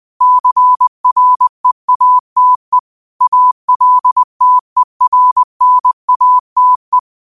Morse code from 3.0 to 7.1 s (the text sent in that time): ALTERNATE